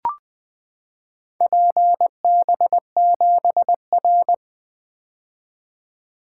Morse code, heard as a keyed tone, about 20 wpm, 700 Hz. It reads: PB7R
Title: Four Letter-Number Combinations - 20wpm